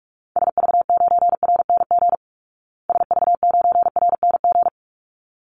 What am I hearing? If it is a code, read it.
Morse code, heard as a keyed tone, about 44 wpm, 700 Hz. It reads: SV9RNG SV9RNG
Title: Call Signs Repeated 3x - 45wpm